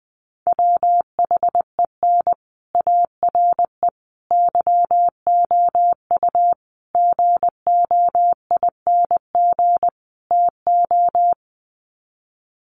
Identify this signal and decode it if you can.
Morse code, heard as a keyed tone, about 19 wpm, 700 Hz. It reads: WHEN ARE YOU GOING TO
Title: Sets of 5 Words Encore2 20wpm